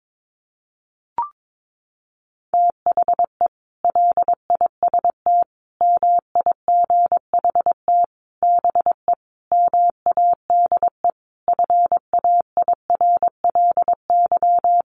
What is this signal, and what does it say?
Morse code, heard as a keyed tone, about 22 wpm, 700 Hz. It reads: THE LIST MIGHT BE MADE FAIRLY
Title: Sentences from Top 1100 Words 22wpm